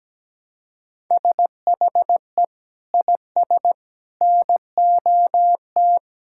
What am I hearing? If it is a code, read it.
Morse code, heard as a keyed tone, about 17 wpm, 700 Hz. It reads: SHE IS NOT
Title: Sentences from Top 1000 Words - Review Part 5 - 17wpm